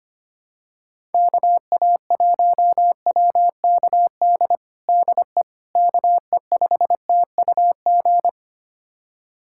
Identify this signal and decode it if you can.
Morse code, heard as a keyed tone, about 25 wpm, 700 Hz. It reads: KA1WKD DE KE5TUG
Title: USA Callsign DE Callsign - 6 Characters - 25wpm